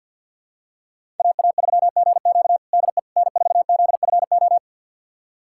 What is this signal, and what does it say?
Morse code, heard as a keyed tone, about 52 wpm, 700 Hz. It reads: AA3GQ DE N4ZFO
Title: USA Callsign DE Callsign - 5 Characters - 50wpm